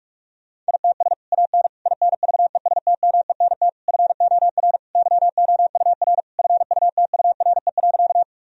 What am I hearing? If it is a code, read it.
Morse code, heard as a keyed tone, about 46 wpm, 700 Hz. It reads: ITS AN INVESTMENT FOR YOUR FUTURE.
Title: Sentences from Top 1000 Words - Review Part 4 - 45wpm